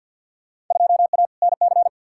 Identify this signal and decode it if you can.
Morse code, heard as a keyed tone, about 52 wpm, 700 Hz. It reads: 2A NC
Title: Field Day Exchanges - 50wpm